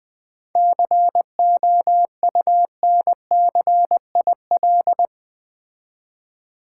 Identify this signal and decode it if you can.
Morse code, heard as a keyed tone, about 20 wpm, 690 Hz. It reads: COUNCIL
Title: 7 Letter Words - First 3 Letters Given - 20wpm